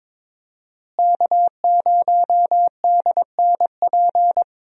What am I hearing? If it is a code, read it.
Morse code, heard as a keyed tone, about 22 wpm, 700 Hz. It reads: K0DNP
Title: USA Call Signs - 5 Characters - Repeated 2x - 22wpm